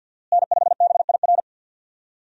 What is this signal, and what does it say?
Morse code, heard as a keyed tone, about 51 wpm, 700 Hz. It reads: N5BIR
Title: USA Call Signs - 5 Characters - Repeated 3x - 50wpm